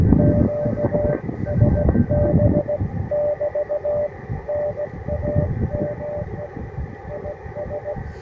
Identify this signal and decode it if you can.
Morse code, heard as a keyed tone, about 17 wpm, 590 Hz, with noise. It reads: QSB <BT> NAME IS
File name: attachments / UR4MCK IZ5GVP